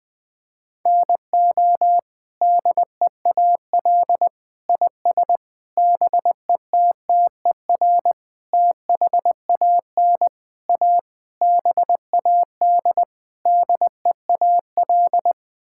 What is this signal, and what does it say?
Morse code, heard as a keyed tone, about 19 wpm, 700 Hz. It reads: NO DEAL IS BETTER THAN A BAD DEAL
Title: Sets of 8 Words - 20wpm